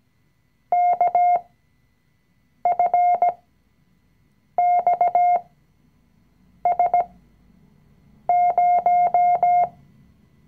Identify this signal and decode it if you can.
Morse code, heard as a keyed tone, about 17 wpm, spaced as for 6 wpm, 710 Hz. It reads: KFXS0
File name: CW-Online-Kurs / Anfänger / Lektion 020